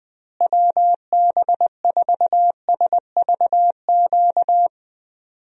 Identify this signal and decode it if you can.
Morse code, heard as a keyed tone, about 20 wpm, 700 Hz. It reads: WB4SVQ
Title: USA Call Signs - 6 Characters - Repeated 3x - 20wpm